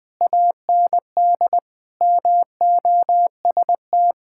Morse code, heard as a keyed tone, about 20 wpm, 700 Hz. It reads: AND MOST
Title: Sentences from Top 1000 Words - Review Part 6 - 20wpm